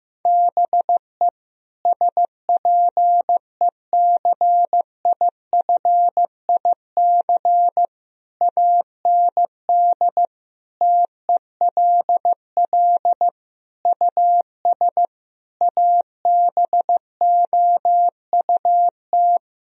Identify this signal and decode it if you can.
Morse code, heard as a keyed tone, about 15 wpm, 700 Hz. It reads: BE SPECIFIC AND TELL US ABOUT
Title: Sentences from Top 1000 Words - Review Part 3 - 15wpm